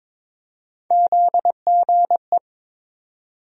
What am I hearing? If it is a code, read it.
Morse code, heard as a keyed tone, about 22 wpm, 700 Hz. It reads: ZGE
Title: USA Call Sign Postfixes - 3 Letters 22wpm